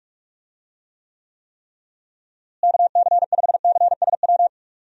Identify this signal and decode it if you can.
Morse code, heard as a keyed tone, about 46 wpm, 700 Hz. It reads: KC5CSW
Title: USA Call Signs - 6 Characters - Repeated 3x - 45wpm